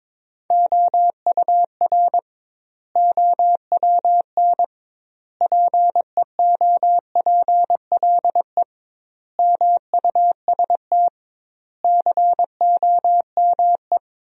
Morse code, heard as a keyed tone, about 22 wpm, 700 Hz. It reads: OUR OWN PEOPLE MUST COME
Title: 2x Word-Spacing - Sentences from Top 300 Words - 22wpm